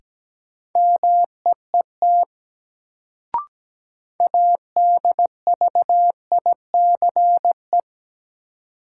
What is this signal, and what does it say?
Morse code, heard as a keyed tone, about 17 wpm, 700 Hz. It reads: MEET ADVICE
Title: Sentences from Top 1100 Words - New Word Drill Alt - 17wpm